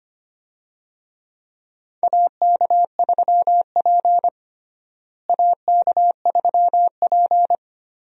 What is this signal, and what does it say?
Morse code, heard as a keyed tone, about 25 wpm, 700 Hz. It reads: AK3P AK3P
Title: USA Call Signs - 4 Characters - Repeated 2x - 25wpm